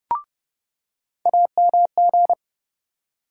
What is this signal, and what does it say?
Morse code, heard as a keyed tone, about 30 wpm, 700 Hz. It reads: AMG
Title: USA Call Sign Postfixes - 3 Letters 30wpm